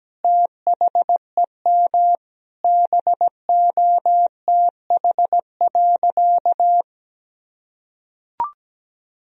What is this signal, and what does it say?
Morse code, heard as a keyed tone, about 17 wpm, 700 Hz. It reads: THEM BOTH.
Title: Sentences from Top 1000 Words - Review Part 1 - 17wpm